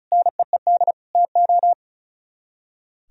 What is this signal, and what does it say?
Morse code, heard as a keyed tone, about 46 wpm, 700 Hz. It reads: NEED TO